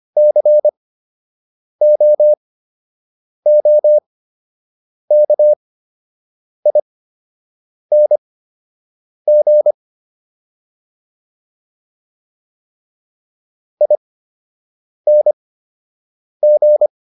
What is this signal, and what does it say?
Morse code, heard as a keyed tone, about 25 wpm, spaced as for 7 wpm, 600 Hz. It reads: COOKING ING